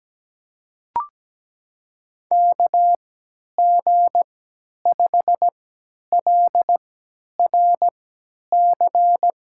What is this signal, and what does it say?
Morse code, heard as a keyed tone, about 17 wpm, spaced as for 10 wpm, 700 Hz. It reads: KG5LRC
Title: USA Call Signs - 3x Character Spacing - 17wpm